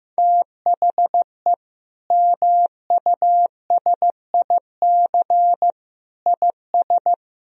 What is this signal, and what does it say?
Morse code, heard as a keyed tone, about 15 wpm, 700 Hz. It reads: THE MUSIC IS
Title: Sentences from Top 1000 Words - Review Part 1 - 15wpm